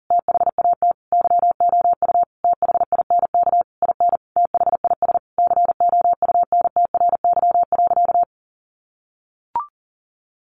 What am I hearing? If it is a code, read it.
Morse code, heard as a keyed tone, about 40 wpm, 700 Hz. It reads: THAT YOU THINK IN THIS COUNTRY.
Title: Sentences from Top 1200 Words 40wpm